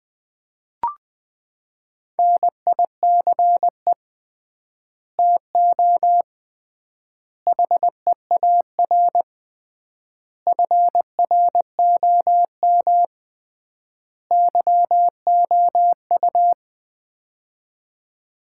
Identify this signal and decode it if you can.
Morse code, heard as a keyed tone, about 20 wpm, 700 Hz. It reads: NICE TO HEAR FROM YOU
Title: Common QSO Phrases - 3x Word-Spacing - 20wpm